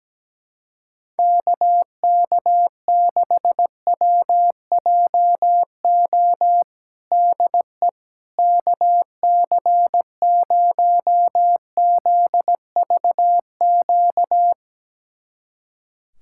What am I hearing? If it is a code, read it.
Morse code, heard as a keyed tone, about 17 wpm, 700 Hz. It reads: KK6WJO DE KC0ZVQ
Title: USA Callsign DE Callsign 17wpm